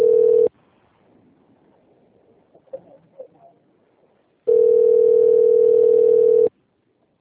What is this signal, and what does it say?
Ringback tone (North America), 440 + 480 Hz.
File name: out- - - - - .36477